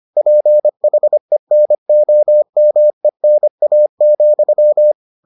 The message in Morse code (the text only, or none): PHENOMENA,